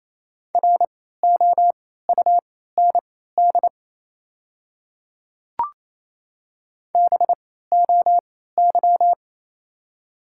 Morse code, heard as a keyed tone, about 28 wpm, spaced as for 16 wpm, 700 Hz. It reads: ROUND BOY
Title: 3x Character Spacing - Top 500 Words - 28wpm